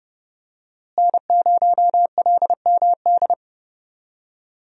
Morse code, heard as a keyed tone, about 32 wpm, 700 Hz. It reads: N0LMD
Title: USA Call Signs - 5 Characters - Repeated 2x - 30wpm